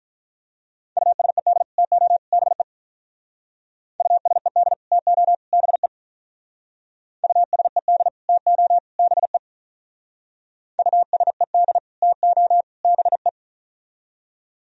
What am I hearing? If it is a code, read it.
Morse code, heard as a keyed tone, about 43 wpm, 700 Hz. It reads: USED TO BE USED TO BE USED TO BE USED TO BE
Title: Speed-Racing - Sets of 3 Words 35wpm